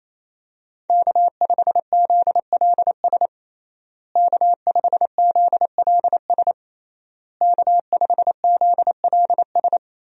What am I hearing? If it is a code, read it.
Morse code, heard as a keyed tone, about 28 wpm, 700 Hz. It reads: K5ZLS K5ZLS K5ZLS
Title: USA Call Signs - 5 Characters - Repeated 3x - 28wpm